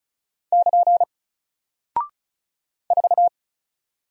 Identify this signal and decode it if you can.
Morse code, heard as a keyed tone, about 36 wpm, 700 Hz. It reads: <KN> 4
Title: Single Character (Letter:Number:Punctuation:Prosign) 35wpm